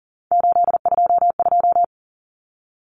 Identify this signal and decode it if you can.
Morse code, heard as a keyed tone, about 40 wpm, 700 Hz. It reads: 822
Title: Common Cut Numbers - 3-Digit 40wpm